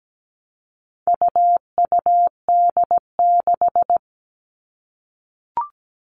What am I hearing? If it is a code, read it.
Morse code, heard as a keyed tone, about 17 wpm, 700 Hz. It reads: UUD6
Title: All Cut Numbers - 4-Digit 17wpm